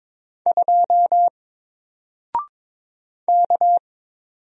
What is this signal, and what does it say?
Morse code, heard as a keyed tone, about 23 wpm, 700 Hz. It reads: 2 K
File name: Single-Letter-Number-22wpm